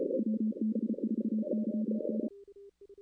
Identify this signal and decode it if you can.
Morse code, heard as a keyed tone, about 29 wpm, 220 Hz. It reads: WB4YN
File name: 2016 arrl dx cw / 2016-ARRL-DX-CW-K5TR- - 0-5